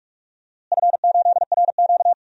Morse code, heard as a keyed tone, about 44 wpm, 700 Hz. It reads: F8RQ
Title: DXCC-100 Call Signs - 4 Characters - Repeated 3x - 45wpm